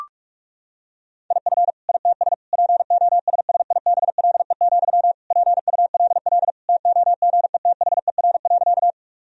Morse code, heard as a keyed tone, about 46 wpm, 700 Hz. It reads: IF ITS POSSIBLE, PULL TOGETHER.
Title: Sentences from Top 1000 Words - Review Part 2 - 45wpm